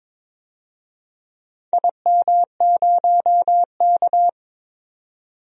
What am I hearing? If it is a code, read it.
Morse code, heard as a keyed tone, about 22 wpm, 700 Hz. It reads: IM0K
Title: DXCC-100 Call Signs - 4 Characters - 22wpm